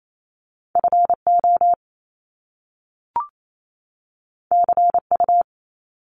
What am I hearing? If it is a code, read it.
Morse code, heard as a keyed tone, about 28 wpm, 700 Hz. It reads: FO CU
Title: Top Two Letter Combinations in Words 28wpm